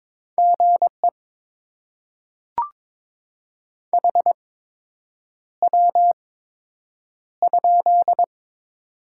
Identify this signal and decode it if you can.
Morse code, heard as a keyed tone, about 22 wpm, spaced as for 8 wpm, 700 Hz. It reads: GE HW?
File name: Top 20 QSO Elements - 8x Character Spacing - 22wpm